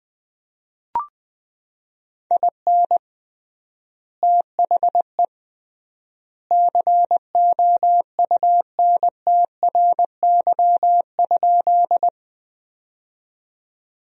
Morse code, heard as a keyed tone, about 21 wpm, 700 Hz. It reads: IN THE COUNTRY?
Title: 3x Word-Spacing - Sentences from Top 300 Words - 20wpm